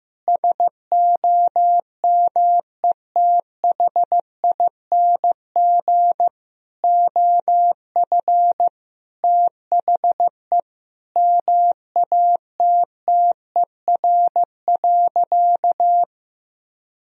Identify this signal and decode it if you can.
Morse code, heard as a keyed tone, about 15 wpm, 700 Hz. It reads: SOMETHING OF THE MATTER.
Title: Sentences from Top 1000 Words - Review Part 3 - 15wpm